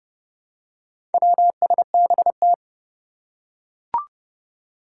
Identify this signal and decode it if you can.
Morse code, heard as a keyed tone, about 32 wpm, 700 Hz. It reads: WSBT
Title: Four Challenging Letter-Number Combinations 30wpm